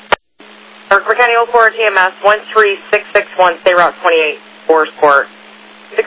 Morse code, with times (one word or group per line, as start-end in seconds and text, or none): none